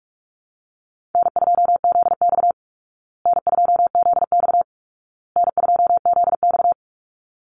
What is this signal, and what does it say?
Morse code, heard as a keyed tone, about 44 wpm, 700 Hz. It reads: N2ZX N2ZX N2ZX
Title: Easy Call Signs Repeated 3x - 45wpm